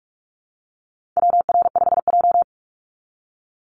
Morse code, heard as a keyed tone, about 44 wpm, 700 Hz. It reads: WR5J
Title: Call Signs - 45wpm